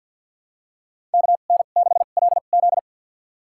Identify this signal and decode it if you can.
Morse code, heard as a keyed tone, about 48 wpm, spaced as for 35 wpm, 700 Hz. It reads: KN6LZ
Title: USA Call Signs - 2x Character Spacing - 50wpm